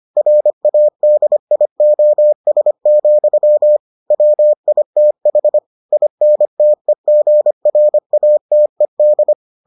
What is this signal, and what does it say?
Morse code, heard as a keyed tone, about 25 wpm, 600 Hz. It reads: RADIOS, WITH INTEGRATED